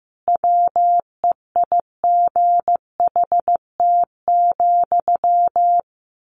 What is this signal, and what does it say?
Morse code, heard as a keyed tone, about 15 wpm, 700 Hz. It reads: WEIGHT,
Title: A Sound Of Thunder 15wpm